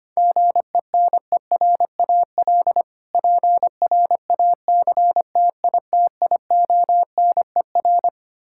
Morse code, heard as a keyed tone, about 25 wpm, 700 Hz. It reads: GENERAL PRACTITIONER
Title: Practice QSO - Exchange 3 - More Info - 25wpm